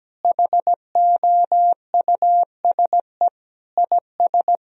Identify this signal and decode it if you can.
Morse code, heard as a keyed tone, about 17 wpm, 700 Hz. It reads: HOUSE IS
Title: Sentences from Top 1000 Words - Review Part 1 - 17wpm